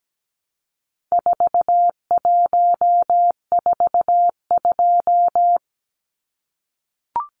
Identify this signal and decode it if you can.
Morse code, heard as a keyed tone, about 17 wpm, 700 Hz. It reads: 4142
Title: Numbers - 4-Digit 17wpm